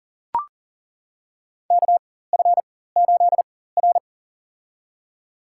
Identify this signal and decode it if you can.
Morse code, heard as a keyed tone, about 40 wpm, spaced as for 19 wpm, 700 Hz. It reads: KF8R
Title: USA Call Signs - 4x Character Spacing - 40wpm